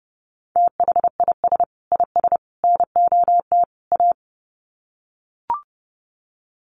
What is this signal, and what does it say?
Morse code, heard as a keyed tone, about 30 wpm, 700 Hz. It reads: THIS IS NOT A
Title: Sets of 4 Words 30wpm